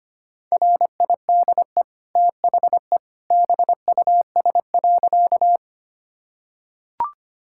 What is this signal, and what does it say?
Morse code, heard as a keyed tone, about 25 wpm, 700 Hz. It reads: RIDE THE BUS.
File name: Easy Sentences with Hints - 25wpm